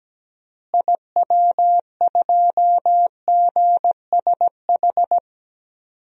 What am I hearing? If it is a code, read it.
Morse code, heard as a keyed tone, about 17 wpm, 700 Hz. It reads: IW2GSH
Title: DXCC-100 Call Signs - 6 Characters - Repeated 3x - 17wpm